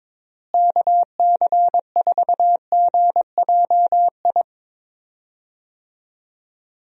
Morse code, heard as a keyed tone, about 22 wpm, 700 Hz. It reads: KC4GJI
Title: USA Call Signs - 6 Characters - Repeated 2x - 22wpm